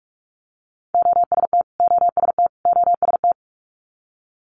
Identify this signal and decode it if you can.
Morse code, heard as a keyed tone, about 46 wpm, 700 Hz. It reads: OST OST OST
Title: Multi-letter Phonemes Repeated 3x 45wpm